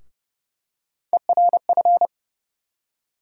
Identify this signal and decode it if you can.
Morse code, heard as a keyed tone, about 30 wpm, 700 Hz. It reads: ERF
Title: USA Call Sign Postfixes - 3 Letters 30wpm